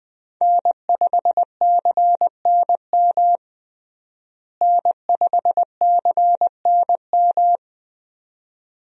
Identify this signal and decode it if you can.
Morse code, heard as a keyed tone, about 21 wpm, 700 Hz. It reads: N5CNM N5CNM
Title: USA Call Signs - 5 Characters - Repeated 2x - 20wpm